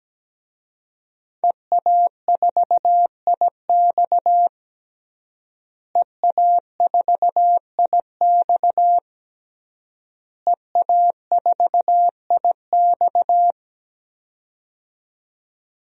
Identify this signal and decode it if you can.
Morse code, heard as a keyed tone, about 17 wpm, 700 Hz. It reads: EA4IX EA4IX EA4IX
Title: DXCC-100 Call Signs - 5 Characters - Repeated 3x - 17wpm